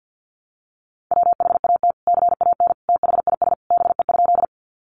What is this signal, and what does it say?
Morse code, heard as a keyed tone, about 50 wpm, 700 Hz. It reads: WHAT CAN THIS BE?
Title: Sentences from Top 100 Words 50wpm